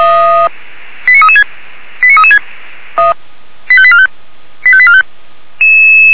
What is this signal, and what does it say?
Five-tone selective call (ZVEI), heard as quick runs of five tones, tones of about 80 ms. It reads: 89336 89336 85845 85845